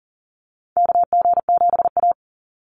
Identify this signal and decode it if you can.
Morse code, heard as a keyed tone, about 40 wpm, 700 Hz. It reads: KG7A
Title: Call Signs - 40wpm